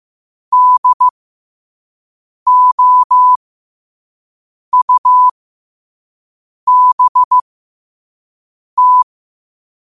Morse code, heard as a keyed tone, about 15 wpm, spaced as for 5 wpm, 1000 Hz. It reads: DOUBT
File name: qotdfiles / 5wpm